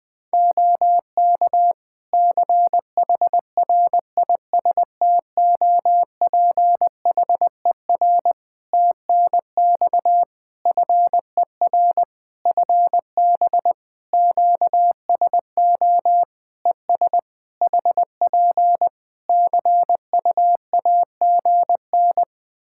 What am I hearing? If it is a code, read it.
Morse code, heard as a keyed tone, about 20 wpm, 700 Hz. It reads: OK CHRISTOPHER TNX FER FB QSO ES HP CUAGN
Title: Practice QSO - Exchange 4 - Ending - 20wpm